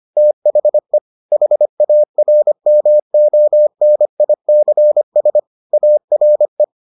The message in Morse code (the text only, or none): THE HARMONICS ARE